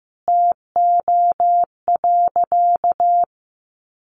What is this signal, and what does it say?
Morse code, heard as a keyed tone, about 15 wpm, 700 Hz. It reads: TO.
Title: A Sound Of Thunder 15wpm